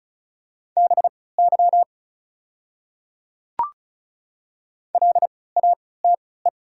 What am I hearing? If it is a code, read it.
Morse code, heard as a keyed tone, about 34 wpm, spaced as for 20 wpm, 700 Hz. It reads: BY LATE
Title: 3x Character Spacing - Top 300 Words - 35wpm